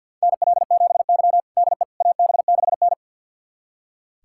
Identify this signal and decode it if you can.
Morse code, heard as a keyed tone, about 47 wpm, 700 Hz. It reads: NL7Y DE AB6N
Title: USA Callsign DE Callsign - 4 Characters - 50wpm